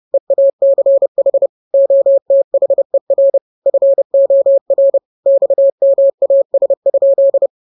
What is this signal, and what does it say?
Morse code, heard as a keyed tone, about 30 wpm, 550 Hz. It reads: EACH OTHER FOR XMAS?
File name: jokes / joke 17 30 0000